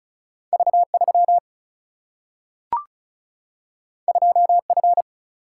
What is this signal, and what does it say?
Morse code, heard as a keyed tone, about 36 wpm, 700 Hz. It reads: V3 2F
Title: Single-Letter Single-Number 35wpm